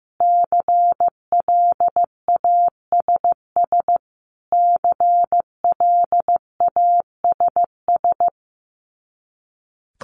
Morse code, heard as a keyed tone, about 14 wpm, 700 Hz. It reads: CLASS CLASS
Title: Top 500 Words Repeated 15wpm